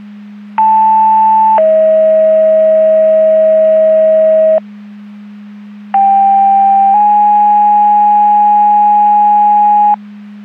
Two-tone page, heard as a one-second tone, then a three-second tone, about 880 then 640 Hz, about 790 then 830 Hz. It is heard from 0.6 to 9.9 s.
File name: Portage 154.385 Paging